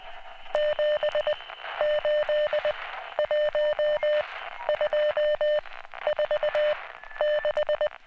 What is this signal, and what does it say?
Morse code, heard as a keyed tone, about 20 wpm, 600 Hz, with noise. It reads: 7 8 1 2 4 6